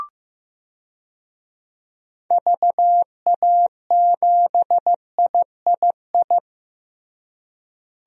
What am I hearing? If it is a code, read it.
Morse code, heard as a keyed tone, about 15 wpm, 700 Hz. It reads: VA7III
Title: DXCC-100 Call Signs - 6 Characters - Repeated 3x - 15wpm